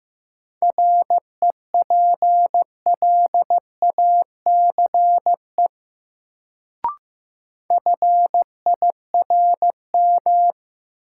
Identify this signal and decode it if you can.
Morse code, heard as a keyed tone, about 15 wpm, 700 Hz. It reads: REPLACE FIRM
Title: Sentences from Top 1400 Words - New Word Drill - 15wpm